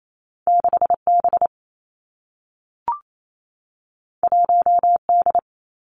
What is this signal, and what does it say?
Morse code, heard as a keyed tone, about 28 wpm, 700 Hz. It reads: 6 B 1 D
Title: Two Challenging Letter-Number Combinations 28wpm